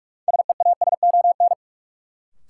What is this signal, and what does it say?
Morse code, heard as a keyed tone, about 41 wpm, 700 Hz. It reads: SEASON
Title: Sentences from Top 1400 Words - New Word Drill - 45wpm